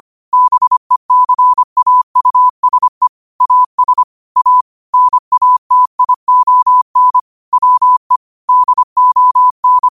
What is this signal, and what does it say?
Morse code, heard as a keyed tone, about 25 wpm, 1000 Hz. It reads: BECAUSE AS A NATION WE DON